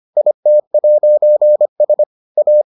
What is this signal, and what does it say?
Morse code, heard as a keyed tone, about 25 wpm, 600 Hz. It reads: IT'S A